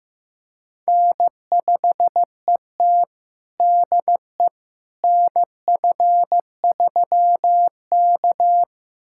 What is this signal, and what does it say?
Morse code, heard as a keyed tone, about 15 wpm, 700 Hz. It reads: N5ET DE NF3K
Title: USA Callsign DE Callsign - 4 Characters - 15wpm